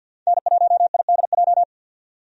Morse code, heard as a keyed tone, about 50 wpm, 700 Hz. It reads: N1IDJ
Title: USA Call Signs - 5 Characters - 50wpm